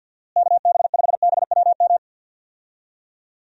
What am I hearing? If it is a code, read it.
Morse code, heard as a keyed tone, about 50 wpm, 700 Hz. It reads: KB5BWM
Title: USA Call Signs - 6 Characters - Repeated 2x - 50wpm